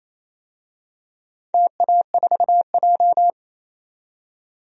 Morse code, heard as a keyed tone, about 28 wpm, 700 Hz. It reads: TA4J
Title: DXCC-100 Call Signs - 4 Characters - 28wpm